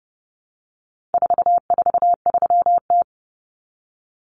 Morse code, heard as a keyed tone, about 30 wpm, 700 Hz. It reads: 443T
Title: Common Cut Numbers - 4-Digit 30wpm